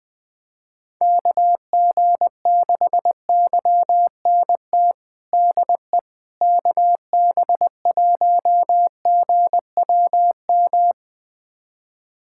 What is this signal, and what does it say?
Morse code, heard as a keyed tone, about 20 wpm, 700 Hz. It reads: KG6YNT DE KB1GWM
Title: USA Callsign DE Callsign - 6 Characters - 20wpm